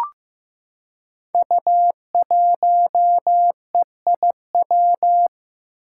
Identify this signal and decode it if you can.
Morse code, heard as a keyed tone, about 15 wpm, 700 Hz. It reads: U1EIW
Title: Five Challenging Letter-Number Combinations 15wpm